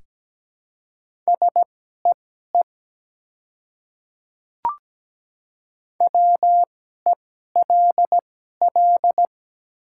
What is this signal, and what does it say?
Morse code, heard as a keyed tone, about 17 wpm, spaced as for 12 wpm, 700 Hz. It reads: SEE WELL
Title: 2x Character Spacing - Top 200 Words - 17wpm